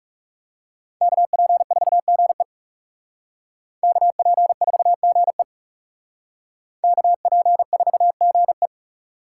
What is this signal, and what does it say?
Morse code, heard as a keyed tone, about 38 wpm, 700 Hz. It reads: KP4GE KP4GE KP4GE